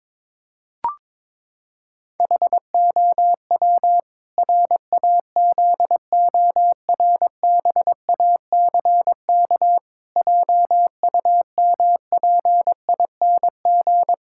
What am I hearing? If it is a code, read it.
Morse code, heard as a keyed tone, about 22 wpm, 700 Hz. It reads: HOW RAZORBACK JUMPING
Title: Pangram Challenge v2 - 22wpm